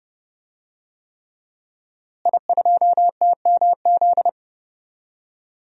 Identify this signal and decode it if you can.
Morse code, heard as a keyed tone, about 30 wpm, 700 Hz. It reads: I2TMZ
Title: DXCC-100 Call Signs - 5 Characters - Repeated 3x - 30wpm